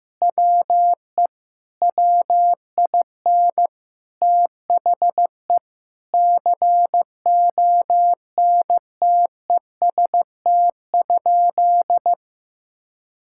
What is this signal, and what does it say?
Morse code, heard as a keyed tone, about 15 wpm, 700 Hz. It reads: WE WIN THE CONTEST?
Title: Sentences from Top 1600 Words 15wpm